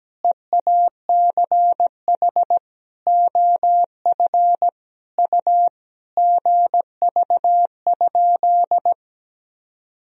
Morse code, heard as a keyed tone, about 17 wpm, 700 Hz. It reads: EACH OF U GV?
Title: Sentences from Top 200 Words and Abbreviations 17wpm